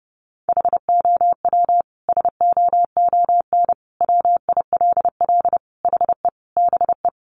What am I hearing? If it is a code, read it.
Morse code, heard as a keyed tone, about 30 wpm, 700 Hz. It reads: HOW SOON WILL HE BE
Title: Sentences from Top 400 Words 30wpm